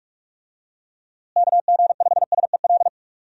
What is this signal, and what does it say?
Morse code, heard as a keyed tone, about 46 wpm, 700 Hz. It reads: KG5SEL
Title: USA Call Signs - 6 Characters - Repeated 3x - 45wpm